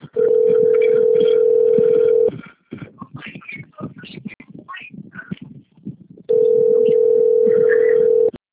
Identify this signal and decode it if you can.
Ringback tone (North America), 440 + 480 Hz.